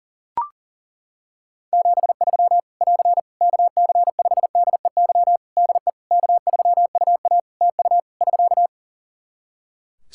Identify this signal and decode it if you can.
Morse code, heard as a keyed tone, about 37 wpm, 700 Hz. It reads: 73 <AR> KC5DEY DE K3UA TU <SK>
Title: Practice QSO - Exchange 4 - Ending - 40wpm